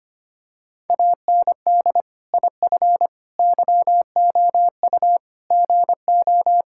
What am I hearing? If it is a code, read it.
Morse code, heard as a keyed tone, about 25 wpm, 700 Hz. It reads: AND IF YOU GO
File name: practice / 25wpm / Sentences with 3-Word Phrases - 25wpm